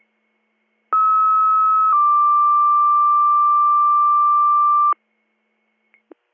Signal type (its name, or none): two-tone page